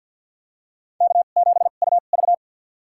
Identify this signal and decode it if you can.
Morse code, heard as a keyed tone, about 52 wpm, spaced as for 37 wpm, 700 Hz. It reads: K7UV
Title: USA Call Signs - 2x Character Spacing - 50wpm